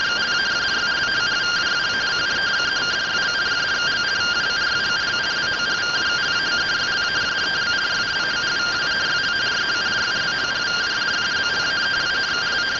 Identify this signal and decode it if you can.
Two-tone teleprinter signal, about 170 Hz shift.